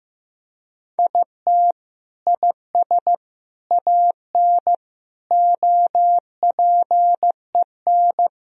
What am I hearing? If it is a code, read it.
Morse code, heard as a keyed tone, about 15 wpm, 700 Hz. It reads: IT IS AN OPEN
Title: Sentences from Top 1000 Words - Review Part 3 - 15wpm